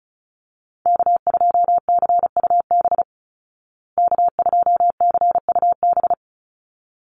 Morse code, heard as a keyed tone, about 34 wpm, 700 Hz. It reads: K2CUB K2CUB
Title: Call Signs Repeated 3x - 35wpm